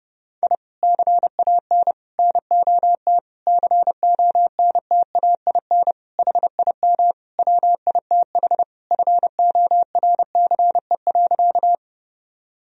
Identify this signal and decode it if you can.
Morse code, heard as a keyed tone, about 30 wpm, 700 Hz. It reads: I CAN NOT CONTAIN HIM WITH FORCE.
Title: Sentences from Top 1000 Words - Review Part 3 - 30wpm